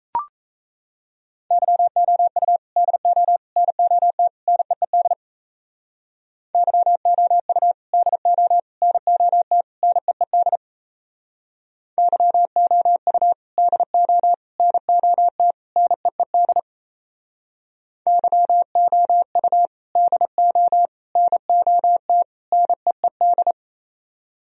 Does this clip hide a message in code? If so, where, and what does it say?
Yes: Morse code from 1.5 to 23.5 s, heard as a keyed tone, about 37 wpm, 700 Hz. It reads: YOU DO NOT NEED YOU DO NOT NEED YOU DO NOT NEED YOU DO NOT NEED